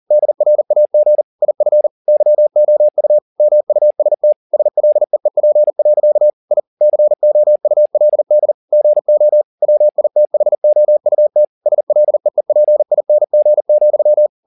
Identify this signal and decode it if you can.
Morse code, heard as a keyed tone, about 40 wpm, 600 Hz. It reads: DRAG IF YOU MUST SLEEP. I COULD GO WITHOUT SLEEPING,
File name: the old man and the sea 2 40wpm